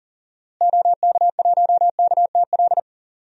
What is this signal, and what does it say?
Morse code, heard as a keyed tone, about 39 wpm, 700 Hz. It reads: OK1KTL
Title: DXCC-100 Call Signs - 6 Characters - Repeated 2x - 40wpm